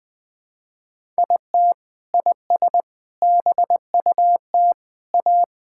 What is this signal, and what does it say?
Morse code, heard as a keyed tone, about 20 wpm, 700 Hz. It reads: IT IS BUT A
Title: Sentences from Top 1000 Words - Review Part 5 - 20wpm